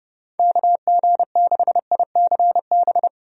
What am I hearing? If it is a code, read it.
Morse code, heard as a keyed tone, about 30 wpm, 700 Hz. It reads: KG6ICB
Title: USA Call Signs - 6 Characters - 30wpm